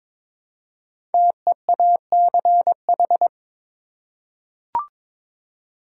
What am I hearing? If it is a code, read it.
Morse code, heard as a keyed tone, about 23 wpm, 700 Hz. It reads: TEACH
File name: 5 Letter Words - First 3 Letters Given - 22wpm